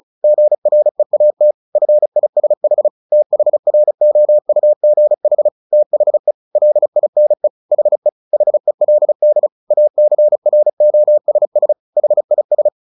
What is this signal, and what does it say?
Morse code, heard as a keyed tone, about 34 wpm, 600 Hz. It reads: GREAT FISH THROUGH THE LINE HE HELD ACROSS HIS
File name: the old man and the sea / the old man and the sea 1 35wpm